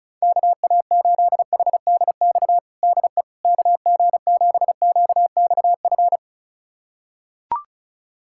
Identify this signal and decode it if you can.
Morse code, heard as a keyed tone, about 36 wpm, 700 Hz. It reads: KA8HDX DE KG7QXF
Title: USA Callsign DE Callsign - 6 Characters - 35wpm